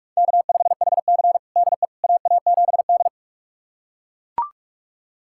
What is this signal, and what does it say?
Morse code, heard as a keyed tone, about 41 wpm, 700 Hz. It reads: K5HC DE AA7D
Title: USA Callsign DE Callsign - 4 Characters - 45wpm